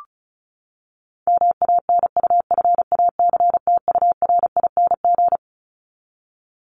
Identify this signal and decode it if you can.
Morse code, heard as a keyed tone, about 34 wpm, 700 Hz. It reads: MANUFACTURING
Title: ING Suffix - Top 100 Words with Base Words 35wpm